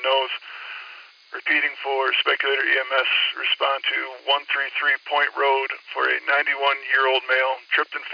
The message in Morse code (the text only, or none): none